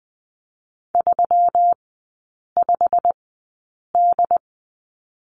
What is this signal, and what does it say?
Morse code, heard as a keyed tone, about 20 wpm, spaced as for 8 wpm, 700 Hz. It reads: 35D
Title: 172 Lesson 28e - Number 9 - Three Characters 20wpm